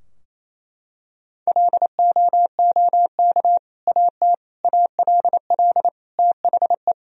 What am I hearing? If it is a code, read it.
Morse code, heard as a keyed tone, about 28 wpm, 700 Hz. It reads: LOOK AT ALL THE
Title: Sentences from Top 1000 Words - Review Part 4 - 28wpm